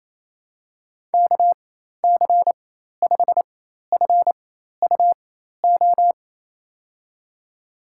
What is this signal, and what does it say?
Morse code, heard as a keyed tone, about 28 wpm, spaced as for 13 wpm, 700 Hz. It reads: KC5FUO
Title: USA Call Signs - 4x Character Spacing - 28wpm